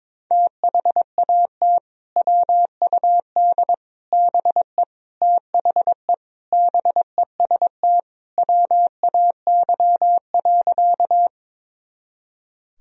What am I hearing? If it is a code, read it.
Morse code, heard as a keyed tone, about 22 wpm, 700 Hz. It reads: THAT WUD BE THE BEST WAY.
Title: Sentences from Top 400 Words and Abbreviations 22wpm